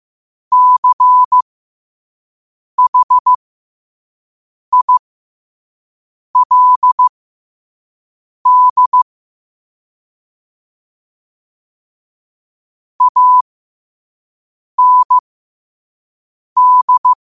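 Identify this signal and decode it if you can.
Morse code, heard as a keyed tone, about 15 wpm, spaced as for 5 wpm, 1000 Hz. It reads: CHILD AND
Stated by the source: Morse code practice at 5 wpm.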